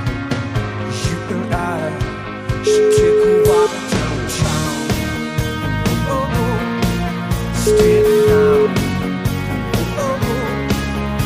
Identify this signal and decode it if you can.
Ringback tone (Europe, 425 Hz), with noise.